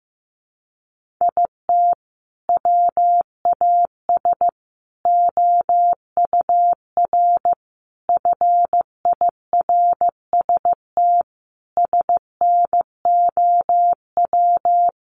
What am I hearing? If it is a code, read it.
Morse code, heard as a keyed tone, about 15 wpm, 700 Hz. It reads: IT WAS OUR FIRST SNOW
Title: Sentences from Top 1500 Words 15wpm